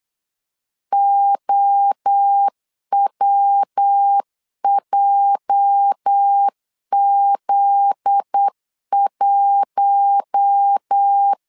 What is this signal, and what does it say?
Morse code, heard as a keyed tone, about 8 wpm, 800 Hz. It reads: OWJZ1
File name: LEZIONI / 8 YJ45 / 8 YJ 8wpm 3313